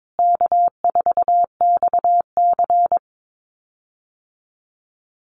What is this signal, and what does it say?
Morse code, heard as a keyed tone, about 22 wpm, 700 Hz. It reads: K4XC
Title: USA Call Signs Repeated 3x 22wpm